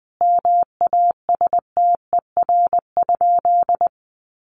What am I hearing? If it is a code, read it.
Morse code, heard as a keyed tone, about 20 wpm, 700 Hz. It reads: MASTER?
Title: Sentences from Top 1300 Words 20wpm